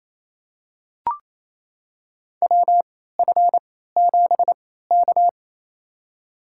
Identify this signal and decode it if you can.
Morse code, heard as a keyed tone, about 28 wpm, spaced as for 16 wpm, 700 Hz. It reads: WF7K
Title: USA Call Signs - 4 Characters - 3x Character Spacing - 28wpm